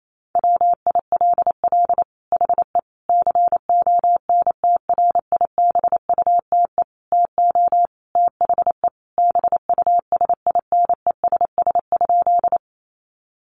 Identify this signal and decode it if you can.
Morse code, heard as a keyed tone, about 28 wpm, 700 Hz. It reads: WILL HE CONTRIBUTE TO THE BUSINESS?
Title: Sentences from Top 1400 Words 28wpm